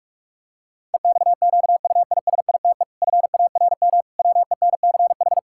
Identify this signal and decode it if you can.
Morse code, heard as a keyed tone, about 48 wpm, 700 Hz. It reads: EXQUISITE FARM WENCH
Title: Pangram Challenge v2 - 45wpm